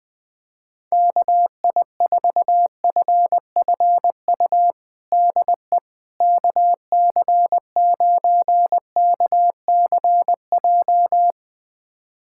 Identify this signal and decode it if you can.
Morse code, heard as a keyed tone, about 20 wpm, 700 Hz. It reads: KI4FFU DE KC9KCJ
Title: USA Callsign DE Callsign 20wpm